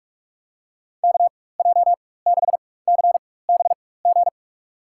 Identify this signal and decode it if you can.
Morse code, heard as a keyed tone, about 46 wpm, spaced as for 21 wpm, 700 Hz. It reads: KJ6CBG
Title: USA Call Signs - 6 Characters - 4x Character Spacing - 45wpm